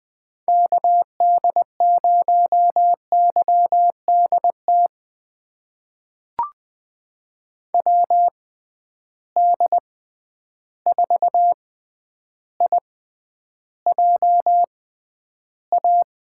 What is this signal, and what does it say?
Morse code, heard as a keyed tone, about 20 wpm, spaced as for 10 wpm, 700 Hz. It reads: KD0YDT WD4IJA
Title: USA Call Signs - 6 Characters - 6x Character Spacing - 20wpm